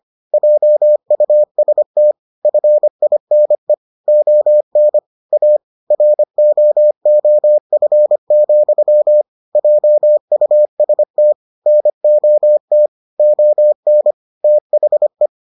Morse code, heard as a keyed tone, about 25 wpm, 600 Hz. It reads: JUST FINE ON A ROOF, JUST NOT ON THE